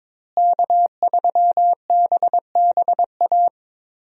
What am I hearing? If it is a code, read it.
Morse code, heard as a keyed tone, about 21 wpm, 700 Hz. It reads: K3BBA
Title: USA Call Signs - 5 Characters - Repeated 3x - 22wpm